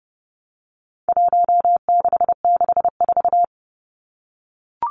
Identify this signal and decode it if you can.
Morse code, heard as a keyed tone, about 30 wpm, 700 Hz. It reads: 1664
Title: Numbers - Contest Style - 4-Digit 30wpm